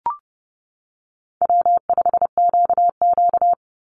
Morse code, heard as a keyed tone, about 30 wpm, 700 Hz. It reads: W5QQ
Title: Easy Call Signs - 30wpm